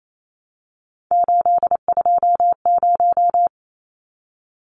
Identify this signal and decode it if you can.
Morse code, heard as a keyed tone, about 28 wpm, 700 Hz. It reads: 820
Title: Numbers - 3-Digit 28wpm